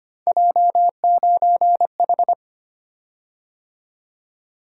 Morse code, heard as a keyed tone, about 25 wpm, 700 Hz. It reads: J9H
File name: Three Letter-Number Combinations - 25wpm